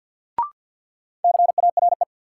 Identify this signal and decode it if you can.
Morse code, heard as a keyed tone, about 53 wpm, 700 Hz. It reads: CARE
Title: Sets of 3 Words - New Word Drill - 50wpm